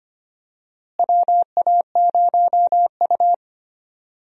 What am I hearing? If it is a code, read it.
Morse code, heard as a keyed tone, about 25 wpm, 700 Hz. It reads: WA0U
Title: USA Call Signs - 4 Characters - Repeated 3x - 25wpm